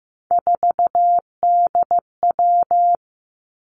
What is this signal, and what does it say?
Morse code, heard as a keyed tone, about 15 wpm, 700 Hz. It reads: 4DW
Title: Three Challenging Letter-Number Combinations 15wpm